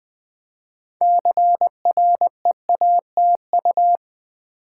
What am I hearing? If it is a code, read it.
Morse code, heard as a keyed tone, about 20 wpm, 690 Hz. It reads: CREATU
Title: Build A Word - 3 Letters At A Time - 8-Letter Words - 20wpm